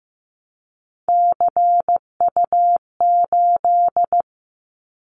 Morse code, heard as a keyed tone, about 15 wpm, 700 Hz. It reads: CU8
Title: Callsign Prefixes for Top 100 DXCC - 15wpm